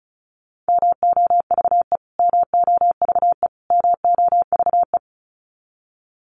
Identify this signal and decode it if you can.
Morse code, heard as a keyed tone, about 34 wpm, 700 Hz. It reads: MOVE MOVE MOVE
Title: Top 500 Words Repeated 35wpm